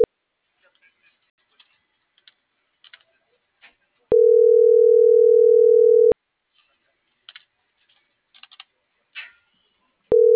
Ringback tone (North America), 440 + 480 Hz.